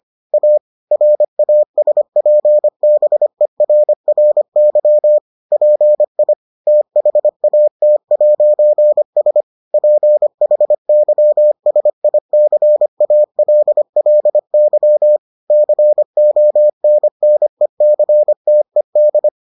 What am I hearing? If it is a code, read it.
Morse code, heard as a keyed tone, about 25 wpm, 600 Hz. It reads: A RASPBERRY PI THAT'S PHYSICALLY CONNECTED